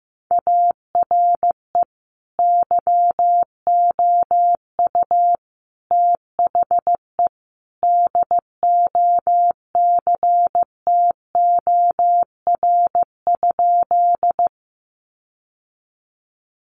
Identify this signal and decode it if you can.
Morse code, heard as a keyed tone, about 15 wpm, 700 Hz. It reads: ARE YOU THE DOCTOR?
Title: Sentences from Top 1500 Words 15wpm